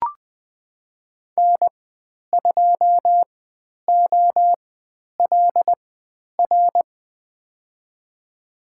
Morse code, heard as a keyed tone, about 20 wpm, spaced as for 10 wpm, 700 Hz. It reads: N2OLR
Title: 109 Lesson 18g - Letter m - Call Signs 20wpm